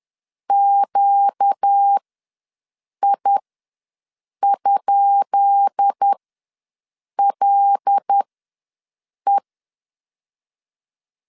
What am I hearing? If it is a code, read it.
Morse code, heard as a keyed tone, about 10 wpm, spaced as for 6 wpm, 800 Hz. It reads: QI?LE
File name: LEZIONI / 10 SIGNS / 10B INT ws 10wpm 4134